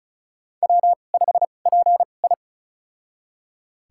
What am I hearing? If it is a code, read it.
Morse code, heard as a keyed tone, about 36 wpm, spaced as for 25 wpm, 700 Hz. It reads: W5PI
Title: USA Call Signs - 2x Character Spacing - 35wpm